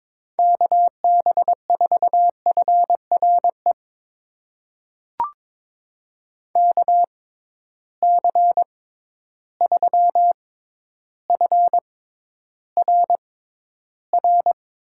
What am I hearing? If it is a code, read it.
Morse code, heard as a keyed tone, about 23 wpm, spaced as for 11 wpm, 700 Hz. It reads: KB4FRE KC3FRR
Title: USA Call Signs - 6 Characters - 6x Character Spacing - 22wpm